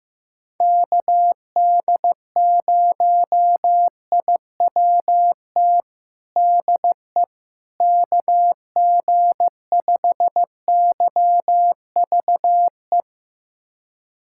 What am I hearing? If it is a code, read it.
Morse code, heard as a keyed tone, about 15 wpm, 700 Hz. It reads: KD0IWT DE KG5YVE
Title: USA Callsign DE Callsign - 6 Characters - 15wpm